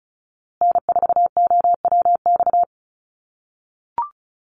Morse code, heard as a keyed tone, about 34 wpm, 700 Hz. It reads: N4OWX
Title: USA Call Signs Repeated 3x 35wpm